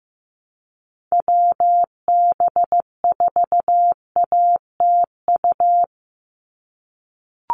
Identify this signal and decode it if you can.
Morse code, heard as a keyed tone, about 15 wpm, 700 Hz. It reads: WB4ATU
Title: USA Call Signs 15wpm